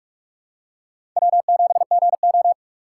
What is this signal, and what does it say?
Morse code, heard as a keyed tone, about 46 wpm, 700 Hz. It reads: W7GO
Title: USA Call Signs - 4 Characters - 2x Character Spacing - 45wpm